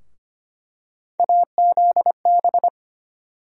Morse code, heard as a keyed tone, about 25 wpm, 700 Hz. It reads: AZB
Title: USA Call Sign Postfixes - 3 Letters 25wpm